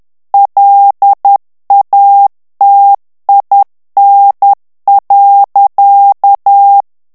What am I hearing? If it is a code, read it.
Morse code, heard as a keyed tone, about 10 wpm, 800 Hz. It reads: LATIN.
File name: morse mp3 / FRANCAIS TEXTE CLAIR 2 VITESSE 10 MOTS MINUTES